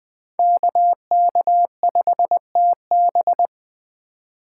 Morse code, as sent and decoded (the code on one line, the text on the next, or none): -.- -.- ..... - -...
KK5TB